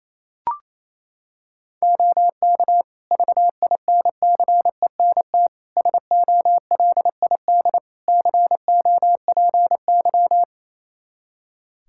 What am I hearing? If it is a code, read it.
Morse code, heard as a keyed tone, about 28 wpm, 700 Hz. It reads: OK VINCENT SOLID COPY
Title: Practice QSO - Exchange 3 - More Info - 28wpm